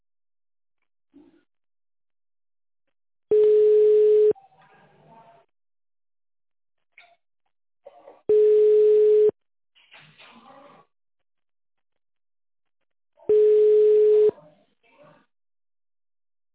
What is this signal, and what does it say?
Ringback tone (Europe, 425 Hz).